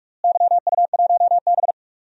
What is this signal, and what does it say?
Morse code, heard as a keyed tone, about 43 wpm, 700 Hz. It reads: YU1B
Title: DXCC-100 Call Signs - 4 Characters - Repeated 3x - 45wpm